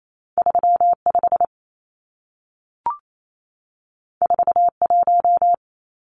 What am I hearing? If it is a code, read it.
Morse code, heard as a keyed tone, about 28 wpm, 700 Hz. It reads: 35 41
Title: Numbers - 2-Digit 28wpm